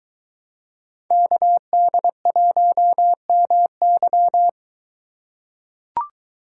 Morse code, heard as a keyed tone, about 22 wpm, 700 Hz. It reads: KD1MY